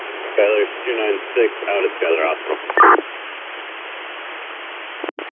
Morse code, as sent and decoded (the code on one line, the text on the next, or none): none